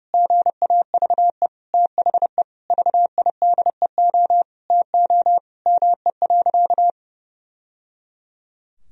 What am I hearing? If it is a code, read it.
Morse code, heard as a keyed tone, about 32 wpm, 700 Hz. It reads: GAVE THE VIDEO TO ME.
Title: Sentences from Top 1000 Words - Review Part 4 - 30wpm